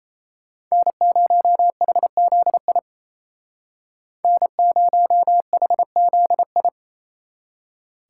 Morse code, heard as a keyed tone, about 35 wpm, 700 Hz. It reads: N0HZI N0HZI
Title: Speed Racing - USA 1x3 Call Signs 28wpm